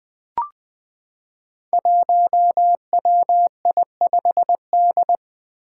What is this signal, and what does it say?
Morse code, heard as a keyed tone, about 20 wpm, 700 Hz. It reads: 1WI5D
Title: Five Challenging Letter-Number Combinations 20wpm